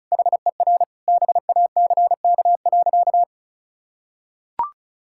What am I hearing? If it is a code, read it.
Morse code, heard as a keyed tone, about 36 wpm, 700 Hz. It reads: HER BACK.
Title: Sentences from Top 1000 Words - Review Part 6 - 35wpm